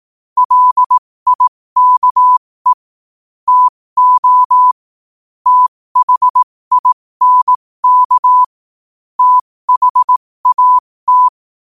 Morse code, heard as a keyed tone, about 18 wpm, spaced as for 15 wpm, 1000 Hz. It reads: LIKE TO THINK THAT
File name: qotdfiles / 15wpm